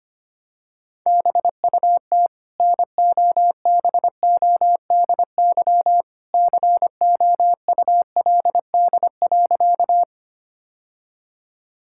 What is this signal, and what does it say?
Morse code, heard as a keyed tone, about 25 wpm, 700 Hz. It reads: BUT NOBODY COULD.
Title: Sentences from Top 1600 Words 25wpm